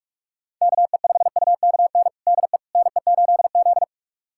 Morse code, heard as a keyed tone, about 46 wpm, 700 Hz. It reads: KE5UKN DE NE8Z
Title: USA Callsign DE Callsign 45wpm